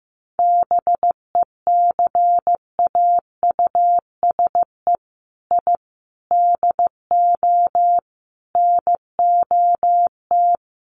Morse code, heard as a keyed tone, about 15 wpm, 700 Hz. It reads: BECAUSE I DO NOT
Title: Sets of 5 Words Encore 15wpm